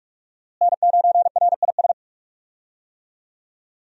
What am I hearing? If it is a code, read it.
Morse code, heard as a keyed tone, about 42 wpm, 700 Hz. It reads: N9RIS
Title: USA Call Signs - 5 Characters - Repeated 2x - 45wpm